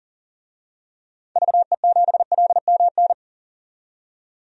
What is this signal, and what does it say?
Morse code, heard as a keyed tone, about 40 wpm, 700 Hz. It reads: VE7LMN
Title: DXCC-100 Call Signs - 6 Characters - 40wpm